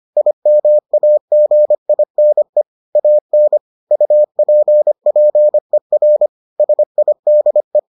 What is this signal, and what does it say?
Morse code, heard as a keyed tone, about 25 wpm, 600 Hz. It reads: IMAGINE AN UPPER SIDE